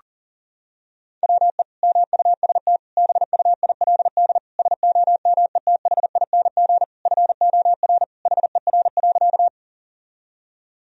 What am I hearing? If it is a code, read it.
Morse code, heard as a keyed tone, about 40 wpm, 700 Hz. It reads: WE MUST BUILD SOMETHING FOR HER.
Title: Sentences from Top 1000 Words - Review Part 1 - 40wpm